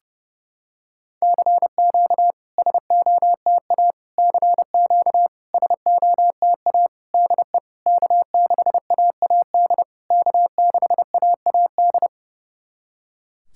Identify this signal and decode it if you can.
Morse code, heard as a keyed tone, about 32 wpm, 700 Hz. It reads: CQ SOTA CQ SOTA DE K6AAD K6AAD
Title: SOTA QSO Practice 30wpm